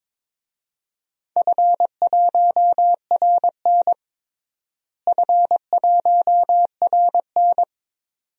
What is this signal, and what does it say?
Morse code, heard as a keyed tone, about 22 wpm, 700 Hz. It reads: F1RN F1RN
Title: DXCC-100 Call Signs - 4 Characters - Repeated 2x - 22wpm